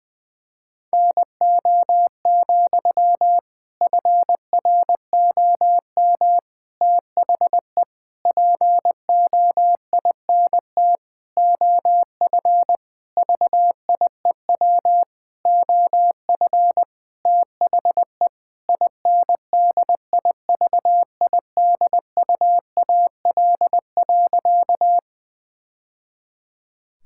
Morse code, heard as a keyed tone, about 20 wpm, 700 Hz. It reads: NO, FROM THE POINT OF VIEW OF THE INDIVIDUAL.
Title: Sentences from Top 1000 Words - Review Part 6 - 20wpm